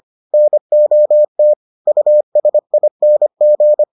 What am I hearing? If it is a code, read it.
Morse code, heard as a keyed tone, about 25 wpm, 600 Hz. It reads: NOT USING